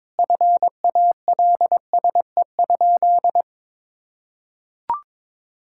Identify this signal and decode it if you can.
Morse code, heard as a keyed tone, about 21 wpm, 700 Hz. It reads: FALSE?
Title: Sentences from Top 1600 Words 22wpm